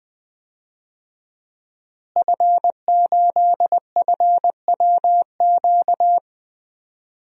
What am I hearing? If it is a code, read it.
Morse code, heard as a keyed tone, about 21 wpm, 700 Hz. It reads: F8FWQ
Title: DXCC-100 Call Signs - 5 Characters - Repeated 3x - 20wpm